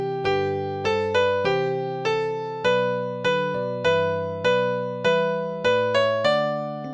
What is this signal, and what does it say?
There is no code.